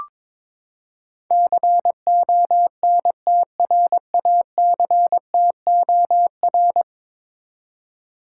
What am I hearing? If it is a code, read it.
Morse code, heard as a keyed tone, about 22 wpm, 700 Hz. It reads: CONTRACTOR
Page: morse-code-ninja-practice-22wpm